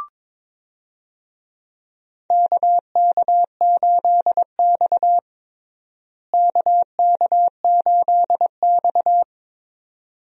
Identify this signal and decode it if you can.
Morse code, heard as a keyed tone, about 22 wpm, 700 Hz. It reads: KK8X KK8X
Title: USA Call Signs - 4 Characters - Repeated 2x - 22wpm